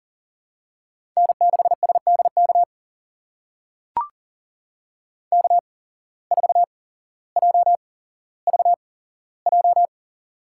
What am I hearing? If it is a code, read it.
Morse code, heard as a keyed tone, about 40 wpm, spaced as for 17 wpm, 700 Hz. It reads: N6SDK K4JVJ
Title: USA Call Signs - 5 Characters - 8x Character Spacing - 40wpm